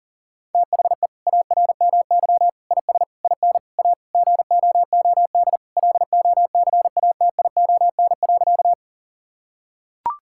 Morse code, heard as a keyed tone, about 40 wpm, 700 Hz. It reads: THE ARMY IS IN A GOOD LOCATION.
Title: Sentences from Top 1000 Words - Review Part 6 - 40wpm